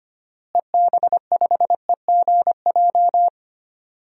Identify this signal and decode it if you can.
Morse code, heard as a keyed tone, about 25 wpm, 700 Hz. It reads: EB5EGJ
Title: DXCC-100 Call Signs - 6 Characters - Repeated 2x - 25wpm